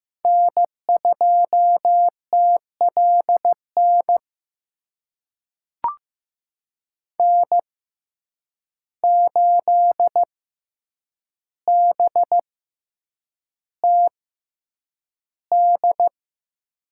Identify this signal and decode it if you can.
Morse code, heard as a keyed tone, about 15 wpm, spaced as for 8 wpm, 700 Hz. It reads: N2TLN N8BTD
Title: USA Call Signs - 5 Characters - 6x Character Spacing - 15wpm